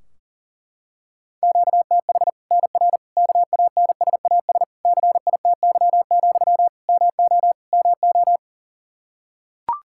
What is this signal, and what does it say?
Morse code, heard as a keyed tone, about 40 wpm, 700 Hz. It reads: QTH NR KANSAS CITY, MO MO
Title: Practice QSO - Exchange 1 - Beginning - 40wpm